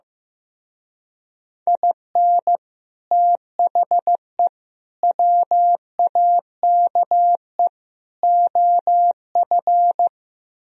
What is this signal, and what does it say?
Morse code, heard as a keyed tone, about 15 wpm, 700 Hz. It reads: IN THE WAKE OF
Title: Speed-Racing - Sets of 4 Words 15wpm